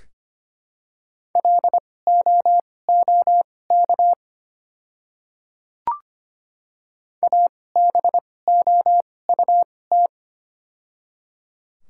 Morse code, heard as a keyed tone, about 25 wpm, spaced as for 18 wpm, 700 Hz. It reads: LOOK ABOUT
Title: 2x Character Spacing - Top 100 Words - 25wpm